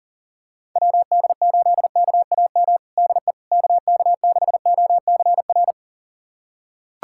Morse code, heard as a keyed tone, about 40 wpm, 700 Hz. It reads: WD8KAM DE KK6OCR
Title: USA Callsign DE Callsign 40wpm